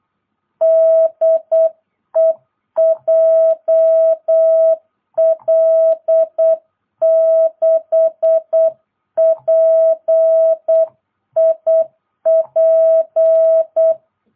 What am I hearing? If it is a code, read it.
Morse code, heard as a keyed tone, about 8 wpm, 650 Hz. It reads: DE JL6PIP